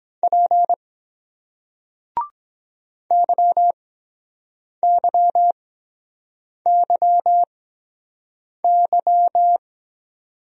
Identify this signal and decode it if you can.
Morse code, heard as a keyed tone, about 23 wpm, spaced as for 7 wpm, 700 Hz. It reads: P YYYY